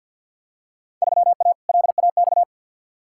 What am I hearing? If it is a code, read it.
Morse code, heard as a keyed tone, about 51 wpm, 700 Hz. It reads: 3A LAX
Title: Field Day Exchanges - 50wpm